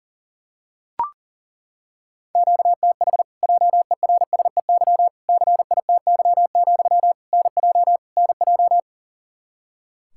Morse code, heard as a keyed tone, about 37 wpm, 700 Hz. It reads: QTH JERSEY CITY, NJ NJ
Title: Practice QSO - Exchange 1 - Beginning - 40wpm